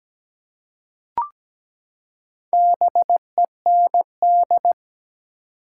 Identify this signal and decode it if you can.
Morse code, heard as a keyed tone, about 17 wpm, 700 Hz. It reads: BEND
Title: Four Letter Words - 17wpm